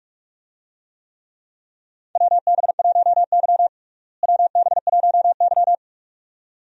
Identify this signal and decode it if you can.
Morse code, heard as a keyed tone, about 46 wpm, 700 Hz. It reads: WB1Y WB1Y
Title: USA Call Signs - 4 Characters - Repeated 2x - 45wpm